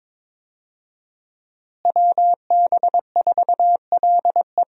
Morse code, heard as a keyed tone, about 21 wpm, 700 Hz. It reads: WB4LE
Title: USA Call Signs - 5 Characters - Repeated 2x - 22wpm